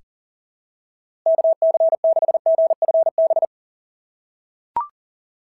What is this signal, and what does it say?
Morse code, heard as a keyed tone, about 40 wpm, 690 Hz. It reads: KC6GFB
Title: SOTA QSO Practice 40wpm